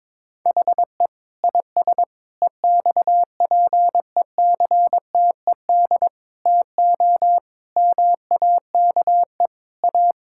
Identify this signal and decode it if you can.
Morse code, heard as a keyed tone, about 22 wpm, 700 Hz. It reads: HE IS EXPECTED TO MAKE A
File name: Sentences with 7-Word Phrases - 22wpm